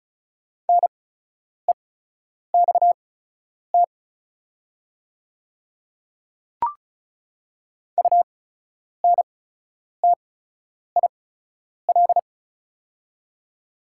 Morse code, heard as a keyed tone, about 34 wpm, spaced as for 10 wpm, 700 Hz. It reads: NEXT UNTIL
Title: 8x Character Spacing - Top 300 Words - 35wpm